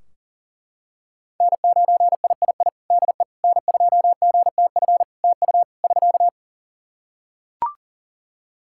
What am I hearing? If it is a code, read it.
Morse code, heard as a keyed tone, about 40 wpm, 700 Hz. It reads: N9III DE N2GTF TU <SK>
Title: Practice QSO - Exchange 4 - Ending - 40wpm